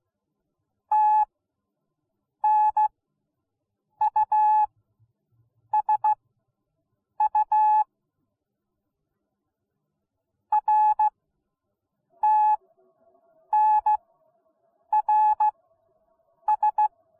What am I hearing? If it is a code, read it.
Morse code, heard as a keyed tone, about 16 wpm, spaced as for 7 wpm, 850 Hz. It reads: TNUSU RTNRS